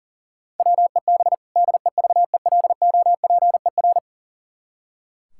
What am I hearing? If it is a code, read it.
Morse code, heard as a keyed tone, about 37 wpm, 700 Hz. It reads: WEB DEVELOPER
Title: Practice QSO - Exchange 3 - More Info - 40wpm